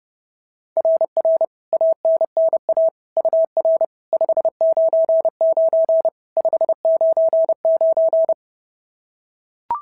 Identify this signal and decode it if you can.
Morse code, heard as a keyed tone, about 30 wpm, 690 Hz. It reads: RR ANNA UR 599 599
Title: SOTA QSO Practice 30wpm